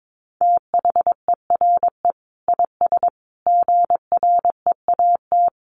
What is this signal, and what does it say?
Morse code, heard as a keyed tone, about 22 wpm, 700 Hz. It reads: THERE IS GREAT
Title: Sentences from Top 1200 Words 22wpm